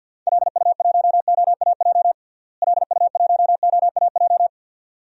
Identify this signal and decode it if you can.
Morse code, heard as a keyed tone, about 52 wpm, 700 Hz. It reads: LU1OAJ LU1OAJ
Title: DXCC-100 Call Signs - 6 Characters - Repeated 2x - 50wpm